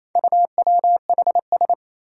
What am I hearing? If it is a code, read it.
Morse code, heard as a keyed tone, about 28 wpm, 700 Hz. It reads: UWHS
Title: Four Challenging Letter-Number Combinations 28wpm